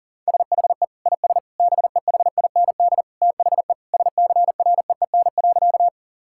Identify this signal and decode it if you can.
Morse code, heard as a keyed tone, about 40 wpm, 700 Hz. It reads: SHE IS BEHIND THE SCREEN.
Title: Sentences from Top 1000 Words - Review Part 6 - 40wpm